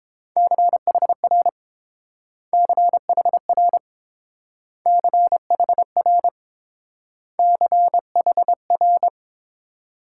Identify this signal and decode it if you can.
Morse code, heard as a keyed tone, about 28 wpm, 700 Hz. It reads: CHR CHR CHR CHR